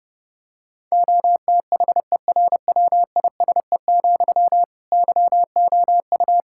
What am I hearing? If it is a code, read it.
Morse code, heard as a keyed tone, about 32 wpm, 700 Hz. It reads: OTHERWISE, YOU
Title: Sentences from Top 1600 Words 30wpm